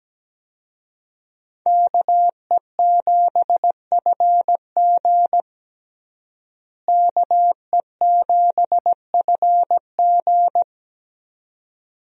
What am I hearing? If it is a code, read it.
Morse code, heard as a keyed tone, about 17 wpm, 700 Hz. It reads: KE7FG KE7FG
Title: USA Call Signs - 5 Characters - Repeated 2x - 17wpm